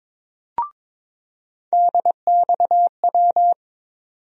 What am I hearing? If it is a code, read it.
Morse code, heard as a keyed tone, about 22 wpm, 700 Hz. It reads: DXW
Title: USA Call Sign Postfixes - 3 Letters 22wpm